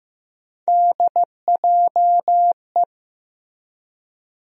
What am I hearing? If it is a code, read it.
Morse code, heard as a keyed tone, about 15 wpm, 700 Hz. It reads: DJE
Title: USA Call Sign Postfixes - 3 Letters 15wpm